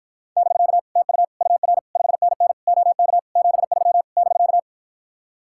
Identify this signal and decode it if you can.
Morse code, heard as a keyed tone, about 53 wpm, 700 Hz. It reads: <BK> TU UR 5NN OK 73 <BK>